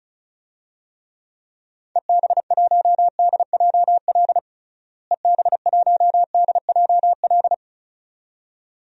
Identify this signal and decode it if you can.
Morse code, heard as a keyed tone, about 36 wpm, 700 Hz. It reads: EB1DJL EB1DJL
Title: DXCC-100 Call Signs - 6 Characters - Repeated 2x - 35wpm